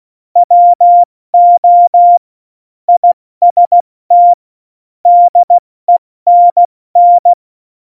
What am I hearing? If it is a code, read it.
Morse code, heard as a keyed tone, about 16 wpm, 700 Hz. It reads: WO IST DENN
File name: Morsekurs / Witze / Witz 16. Oktober 2018, 70 F80